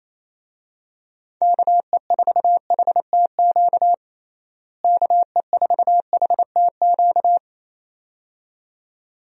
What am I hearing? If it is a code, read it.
Morse code, heard as a keyed tone, about 28 wpm, 700 Hz. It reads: KE4HTQ KE4HTQ
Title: USA Call Signs - 6 Characters - Repeated 2x - 28wpm